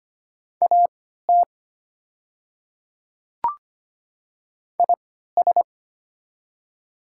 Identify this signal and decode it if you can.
Morse code, heard as a keyed tone, about 25 wpm, spaced as for 14 wpm, 700 Hz. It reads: AT IS
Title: 3x Character Spacing - Top 100 Words - 25wpm